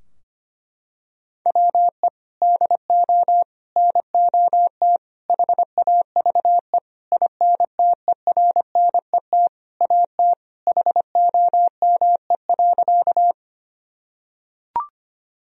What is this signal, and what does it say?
Morse code, heard as a keyed tone, about 25 wpm, 700 Hz. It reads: WE DO NOT HAVE INTERNET AT HOME.
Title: Sentences from Top 1000 Words - Review Part 4 - 25wpm